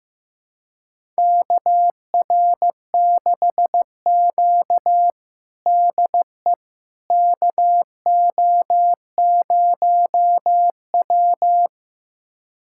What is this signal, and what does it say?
Morse code, heard as a keyed tone, about 15 wpm, 700 Hz. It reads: KR6Q DE KO0W
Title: USA Callsign DE Callsign - 4 Characters - 15wpm